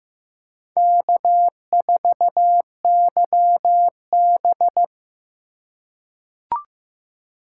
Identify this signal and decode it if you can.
Morse code, heard as a keyed tone, about 15 wpm, 700 Hz. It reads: K4YB
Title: USA Call Signs - 2x Character Spacing - 15wpm